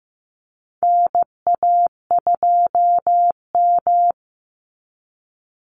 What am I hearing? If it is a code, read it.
Morse code, heard as a keyed tone, about 15 wpm, 700 Hz. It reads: NA2M
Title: Call Signs Repeated 2x - 15wpm